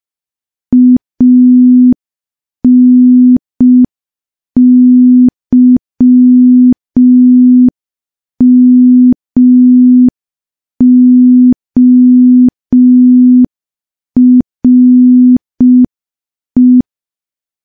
Morse code, heard as a keyed tone, about 5 wpm, 250 Hz. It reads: ANYMORE
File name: morse / morse-training-05WPM